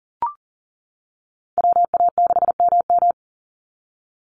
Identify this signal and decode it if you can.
Morse code, heard as a keyed tone, about 39 wpm, 700 Hz. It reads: WA6MM
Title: NA SOTA Chaser Callsigns 40wpm